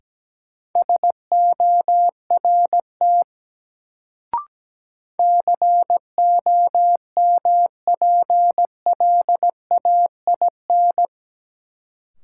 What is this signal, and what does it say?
Morse code, heard as a keyed tone, about 17 wpm, 700 Hz. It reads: SORT COMPLAIN
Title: Sentences from Top 1300 Words - New Word Drill Alt - 17wpm